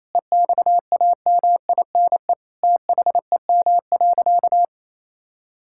Morse code, heard as a keyed tone, about 28 wpm, 700 Hz. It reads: EXAMINE THEM.
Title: Sentences from Top 1600 Words 28wpm